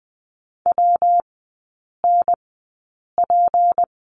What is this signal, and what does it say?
Morse code, heard as a keyed tone, about 20 wpm, spaced as for 8 wpm, 700 Hz. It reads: WNP
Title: 205 Lesson 33e - Letter j - Three Characters 20wpm